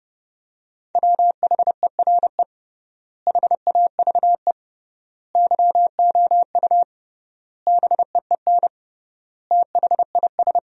Morse code, heard as a keyed tone, about 30 wpm, 700 Hz. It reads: WHERE HAVE YOU BEEN THIS
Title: 3x Word-Spacing - Sentences from Top 300 Words - 30wpm